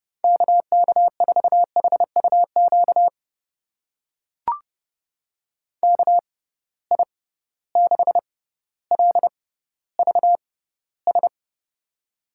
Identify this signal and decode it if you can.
Morse code, heard as a keyed tone, about 30 wpm, spaced as for 15 wpm, 700 Hz. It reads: KK4HUQ KI6LVS
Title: USA Call Signs - 6 Characters - 6x Character Spacing - 30wpm